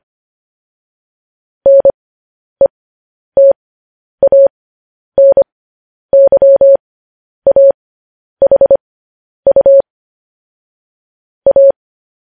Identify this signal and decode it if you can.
Morse code, heard as a keyed tone, about 25 wpm, spaced as for 10 wpm, 550 Hz. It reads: NETANYAHU A
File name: podcast / News headlines-1x-25x10